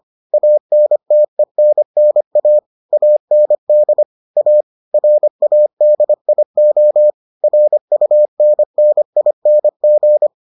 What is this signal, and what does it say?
Morse code, heard as a keyed tone, about 25 wpm, 600 Hz. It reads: ANTENNA AND A RADIO RUNNING